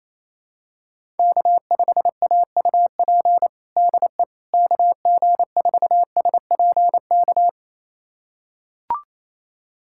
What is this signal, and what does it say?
Morse code, heard as a keyed tone, about 28 wpm, 700 Hz. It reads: K5AUP DE KG4SPK
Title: USA Callsign DE Callsign 28wpm